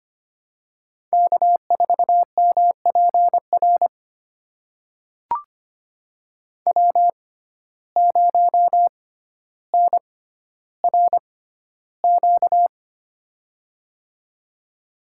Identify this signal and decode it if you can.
Morse code, heard as a keyed tone, about 25 wpm, spaced as for 13 wpm, 700 Hz. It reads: K4MPR W0NRQ
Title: USA Call Signs - 5 Characters - 6x Character Spacing - 25wpm